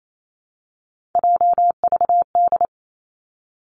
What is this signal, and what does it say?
Morse code, heard as a keyed tone, about 28 wpm, 700 Hz. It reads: JVD
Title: Three Challenging Letter-Number Combinations 28wpm